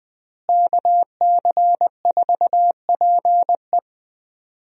Morse code, heard as a keyed tone, about 21 wpm, 700 Hz. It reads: KC4PE
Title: USA Call Signs - 5 Characters - Repeated 2x - 20wpm